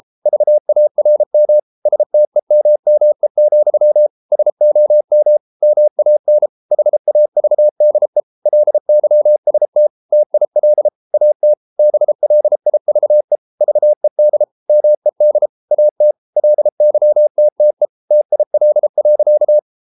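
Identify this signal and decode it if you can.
Morse code, heard as a keyed tone, about 33 wpm, 600 Hz. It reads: VARM STEMME, SOM MAN HAVDE LYST TIL AT BLIVE VED MED AT LYTTE TIL.
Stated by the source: Morse code at 33 wpm, QRQ.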